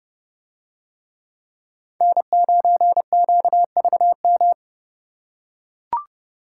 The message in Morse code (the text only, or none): N9QVM